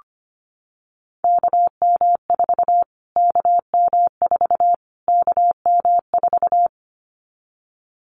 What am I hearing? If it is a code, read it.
Morse code, heard as a keyed tone, about 25 wpm, 700 Hz. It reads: KM4 KM4 KM4
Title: Callsign Prefixes for Top 25 DXCC - Repeated 3x - 25wpm